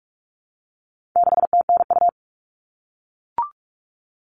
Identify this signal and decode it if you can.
Morse code, heard as a keyed tone, about 44 wpm, 700 Hz. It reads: 6TNU
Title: All Cut Numbers - 4-Digit 45wpm